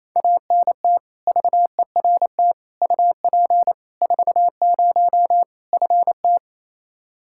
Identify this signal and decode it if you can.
Morse code, heard as a keyed tone, about 27 wpm, 700 Hz. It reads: ANT VERT UP 40 FT
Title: Practice QSO - Exchange 2 - Info - 28wpm